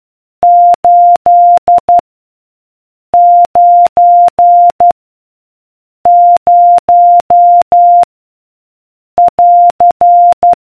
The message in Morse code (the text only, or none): 8 9 0 <AR>